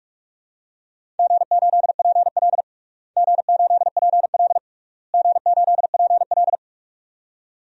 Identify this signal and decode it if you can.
Morse code, heard as a keyed tone, about 41 wpm, 700 Hz. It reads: G8PL G8PL G8PL
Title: DXCC-100 Call Signs - 4 Characters - Repeated 3x - 45wpm